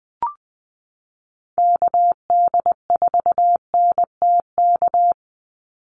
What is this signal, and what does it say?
Morse code, heard as a keyed tone, about 20 wpm, 700 Hz. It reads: KD4NTK
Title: USA Call Signs 20wpm